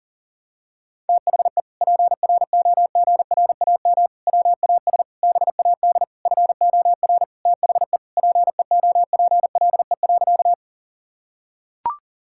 Morse code, heard as a keyed tone, about 37 wpm, 700 Hz. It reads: THE PROGRAM WAS BAD FOR THE PEOPLE.
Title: Sentences from Top 1000 Words - Review Part 5 - 40wpm